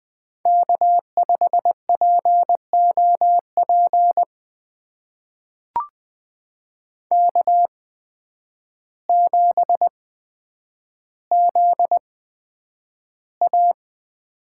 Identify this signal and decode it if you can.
Morse code, heard as a keyed tone, about 21 wpm, spaced as for 9 wpm, 700 Hz. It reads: K5POP K7ZA
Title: USA Call Signs - 5 Characters - 8x Character Spacing - 20wpm